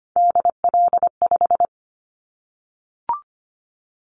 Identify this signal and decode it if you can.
Morse code, heard as a keyed tone, about 25 wpm, 700 Hz. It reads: DL5
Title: Callsign Prefixes for Top 50 DXCC - 25wpm